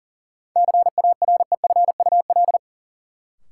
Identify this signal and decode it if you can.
Morse code, heard as a keyed tone, about 40 wpm, 700 Hz. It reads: CAREFUL
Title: Sentences from Top 1600 Words - New Word Drill Alt - 40wpm